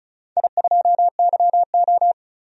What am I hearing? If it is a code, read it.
Morse code, heard as a keyed tone, about 36 wpm, 700 Hz. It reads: I2YO
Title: DXCC-100 Call Signs - 4 Characters - 35wpm